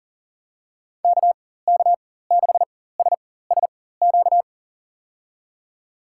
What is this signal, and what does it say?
Morse code, heard as a keyed tone, about 40 wpm, spaced as for 19 wpm, 700 Hz. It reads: KK6SSQ
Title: USA Call Signs - 6 Characters - 4x Character Spacing - 40wpm